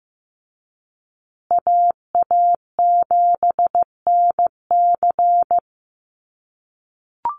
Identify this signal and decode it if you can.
Morse code, heard as a keyed tone, about 15 wpm, 700 Hz. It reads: AA7NC
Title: USA Call Signs Repeated 3x 15wpm